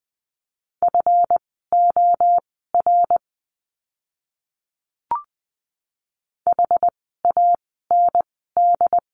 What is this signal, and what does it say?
Morse code, heard as a keyed tone, about 20 wpm, spaced as for 14 wpm, 700 Hz. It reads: FOR HAND
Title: 237 Lesson 38f - Letter q - Words 20wpm